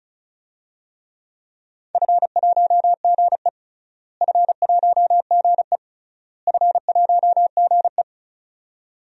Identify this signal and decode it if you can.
Morse code, heard as a keyed tone, about 36 wpm, 700 Hz. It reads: F1GE F1GE F1GE
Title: DXCC-100 Call Signs - 4 Characters - Repeated 3x - 35wpm